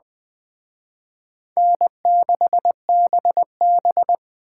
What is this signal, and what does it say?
Morse code, heard as a keyed tone, about 20 wpm, 700 Hz. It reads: N6BB